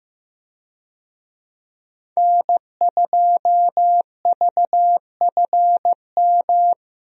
Morse code, heard as a keyed tone, about 15 wpm, 700 Hz. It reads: N2VFM
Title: USA Call Signs - 5 Characters - Repeated 3x - 15wpm